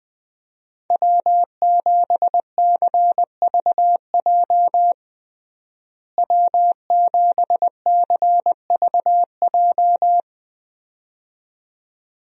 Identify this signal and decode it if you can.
Morse code, heard as a keyed tone, about 20 wpm, 700 Hz. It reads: W7CVJ W7CVJ
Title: USA Call Signs - 5 Characters - Repeated 2x - 20wpm